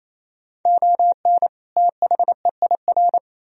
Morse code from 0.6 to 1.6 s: ON

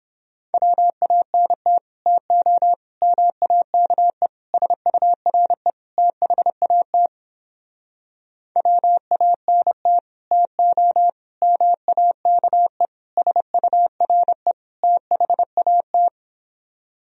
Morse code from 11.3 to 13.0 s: MAKE

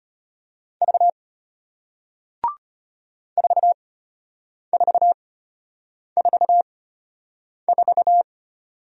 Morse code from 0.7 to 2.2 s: V

